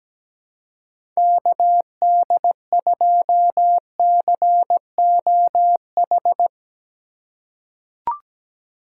Morse code, heard as a keyed tone, about 17 wpm, 700 Hz. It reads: KD2COH
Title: USA Call Signs - 6 Characters - 6x Character Spacing - 17wpm